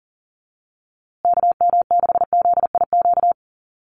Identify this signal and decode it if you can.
Morse code, heard as a keyed tone, about 38 wpm, 700 Hz. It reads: KM6ZIQ
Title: USA Call Signs Repeated 2x 40wpm